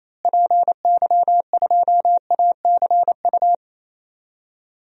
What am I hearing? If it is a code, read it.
Morse code, heard as a keyed tone, about 28 wpm, 700 Hz. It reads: PY2ACU
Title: DXCC-100 Call Signs - 6 Characters - Repeated 3x - 28wpm